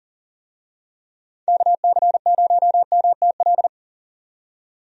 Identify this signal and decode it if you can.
Morse code, heard as a keyed tone, about 40 wpm, 700 Hz. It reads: KC0MTL
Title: USA Call Signs - 6 Characters - 40wpm